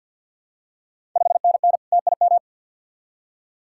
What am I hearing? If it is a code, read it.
Morse code, heard as a keyed tone, about 51 wpm, 690 Hz. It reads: 5NN TIM
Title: Ontario QSO Party Exchange 50wpm